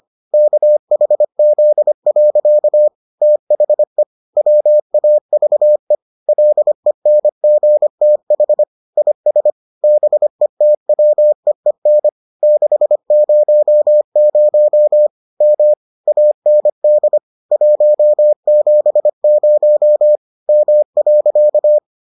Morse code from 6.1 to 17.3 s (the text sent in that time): LENGTH IS BETWEEN 600 M AND